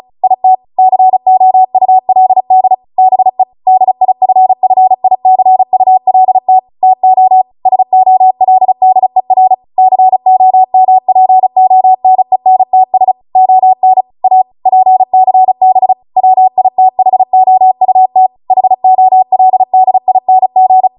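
Morse code, heard as a keyed tone, about 36 wpm, 750 Hz. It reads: IT COULD BE DIFFICULT TO SOLDER COMPONENTS ON A PCB WITHOUT HOLDING
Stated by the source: Amateur radio Morse code practice archive transmission at 35 wpm.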